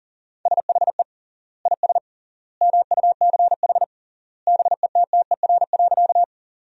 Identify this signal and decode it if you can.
Morse code, heard as a keyed tone, about 40 wpm, 700 Hz. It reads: SHE IS MUCH BETTER.
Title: 3x Word-Spacing - Sentences from Top 400 Words - 40wpm